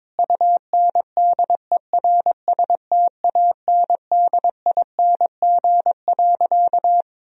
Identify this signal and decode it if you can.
Morse code, heard as a keyed tone, about 22 wpm, 700 Hz. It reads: UNDERSTANDING.
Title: Sentences from Top 1000 Words - Review Part 3 - 22wpm